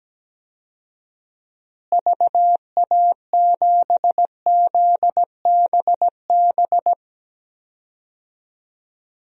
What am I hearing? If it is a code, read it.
Morse code, heard as a keyed tone, about 17 wpm, 700 Hz. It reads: VA7ZBB
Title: DXCC-100 Call Signs - 6 Characters - Repeated 2x - 17wpm